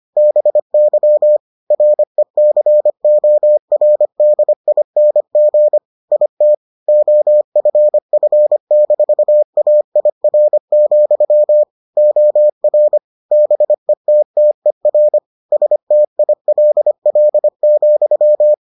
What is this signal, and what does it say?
Morse code, heard as a keyed tone, about 25 wpm, 600 Hz. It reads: BY RECORDING IT OFF-AIR, OR BETTER STILL,